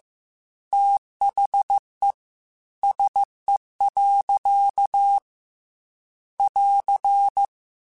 Morse code, heard as a keyed tone, about 15 wpm, 800 Hz. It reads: THE SE. <AR>
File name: Weather Sets / Weather Sets-mp3s / Weather 4 17 @13